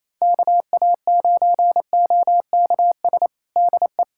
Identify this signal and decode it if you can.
Morse code, heard as a keyed tone, about 27 wpm, 700 Hz. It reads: KA9OKS DE
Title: USA Callsign DE Callsign - 6 Characters - 28wpm